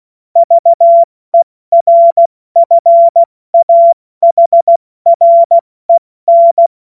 Morse code, heard as a keyed tone, about 16 wpm, 680 Hz. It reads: VERFAHREN
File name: Morsekurs / Witze / Witz 25. Oktober 2022 70 F80